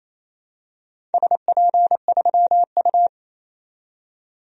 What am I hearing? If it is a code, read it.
Morse code, heard as a keyed tone, about 27 wpm, 700 Hz. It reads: SP3U
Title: DXCC-100 Call Signs - 4 Characters - 28wpm